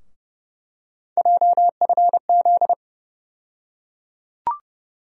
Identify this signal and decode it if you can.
Morse code, heard as a keyed tone, about 29 wpm, 700 Hz. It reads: JFZ
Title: USA Call Sign Postfixes - 3 Letters 30wpm